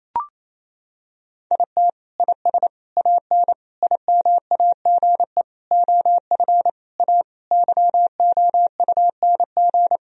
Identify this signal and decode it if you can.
Morse code, heard as a keyed tone, about 28 wpm, 700 Hz. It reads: IT IS AN IMAGE OF A YOUNG
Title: Sentences from Top 1000 Words - Review Part 4 - 28wpm